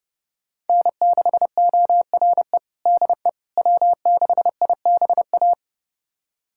Morse code, heard as a keyed tone, about 30 wpm, 700 Hz. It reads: N6ORE DE W6IBA
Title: USA Callsign DE Callsign - 5 Characters - 30wpm